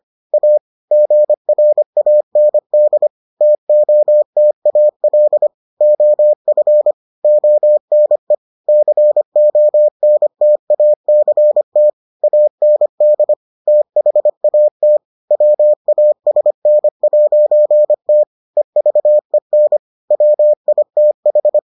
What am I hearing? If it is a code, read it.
Morse code, heard as a keyed tone, about 25 wpm, 600 Hz. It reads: A GRAND TOTAL OF ONE CONTACT AND THAT WASN'T EVEN WITH